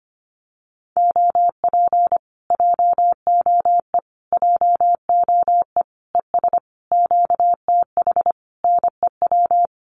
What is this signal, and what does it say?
Morse code, heard as a keyed tone, about 25 wpm, 700 Hz. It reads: OP JOE JOE ES QTH NEW
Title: Common QSO Phrases Encore 25wpm